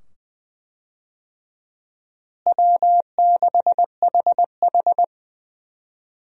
Morse code, heard as a keyed tone, about 20 wpm, 700 Hz. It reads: W6HH
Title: USA Call Signs - 4 Characters - Repeated 2x - 20wpm